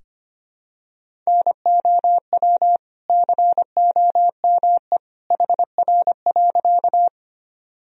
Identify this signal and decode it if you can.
Morse code, heard as a keyed tone, about 25 wpm, 700 Hz. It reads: NOW COME HR.
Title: Sentences from Top 200 Words and Abbreviations 25wpm